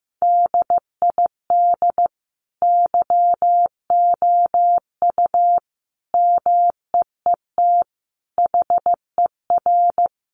Morse code, heard as a keyed tone, about 15 wpm, 700 Hz. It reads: DID YOU MEET HER
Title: Sentences from Top 1100 Words 15wpm